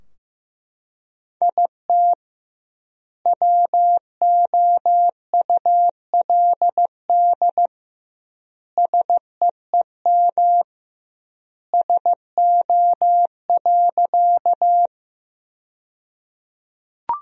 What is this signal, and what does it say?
Morse code, heard as a keyed tone, about 15 wpm, 700 Hz. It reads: IT WOULD SEEM SO.
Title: 2x Word-Spacing - Sentences from Top 300 Words - 15wpm